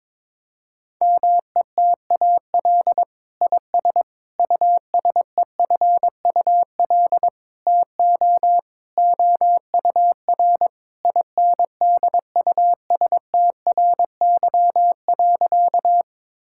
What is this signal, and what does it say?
Morse code, heard as a keyed tone, about 22 wpm, 700 Hz. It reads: METAL IS USEFUL TO OUR INDUSTRY.
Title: Sentences from Top 1000 Words - Review Part 6 - 22wpm